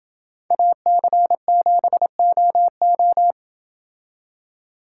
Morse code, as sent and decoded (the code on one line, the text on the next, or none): .- -.-. --... --- ---
AC7OO